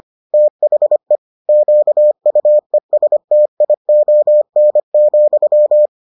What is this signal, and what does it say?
Morse code, heard as a keyed tone, about 25 wpm, 600 Hz. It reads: THE QUESTION,